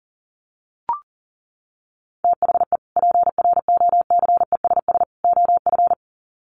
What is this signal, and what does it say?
Morse code, heard as a keyed tone, about 40 wpm, 700 Hz. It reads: THE PROCESS OF
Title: Sets of 3 Words 40wpm